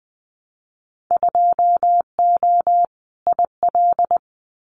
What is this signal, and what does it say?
Morse code, heard as a keyed tone, about 20 wpm, 700 Hz. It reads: 2O IL
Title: Winter Field Day Exchanges 20wpm